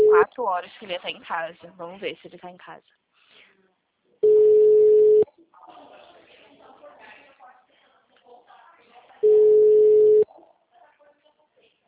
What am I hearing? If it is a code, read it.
Ringback tone (Europe, 425 Hz), with speech.